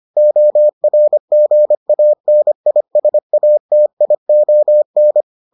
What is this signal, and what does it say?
Morse code, heard as a keyed tone, about 25 wpm, 600 Hz. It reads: ORGANISATION